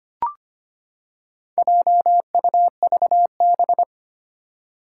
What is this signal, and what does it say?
Morse code, heard as a keyed tone, about 25 wpm, 700 Hz. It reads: JUVB
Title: Four Challenging Letter-Number Combinations 25wpm